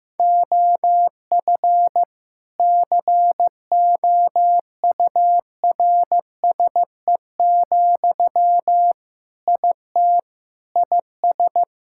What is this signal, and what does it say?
Morse code, heard as a keyed tone, about 16 wpm, 700 Hz. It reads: OF COURSE, IT IS
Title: Sentences from Top 1600 Words 15wpm